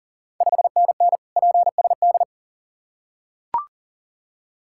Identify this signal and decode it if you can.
Morse code, heard as a keyed tone, about 38 wpm, 690 Hz. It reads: 5NN PSD
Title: Ontario QSO Party Exchange 40wpm